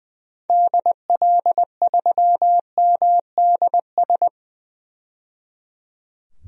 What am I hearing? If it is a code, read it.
Morse code, heard as a keyed tone, about 20 wpm, 700 Hz. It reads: DL3MDS
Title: DXCC-100 Call Signs - 6 Characters - 20wpm